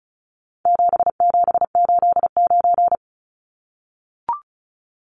Morse code, heard as a keyed tone, about 34 wpm, 700 Hz. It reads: 7789
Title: Numbers - 4-Digit 35wpm